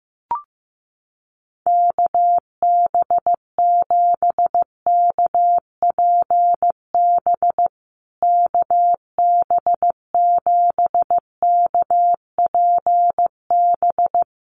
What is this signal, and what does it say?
Morse code, heard as a keyed tone, about 15 wpm, 700 Hz. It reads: KB7KPB KB7KPB
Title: USA Call Signs Repeated 3x 15wpm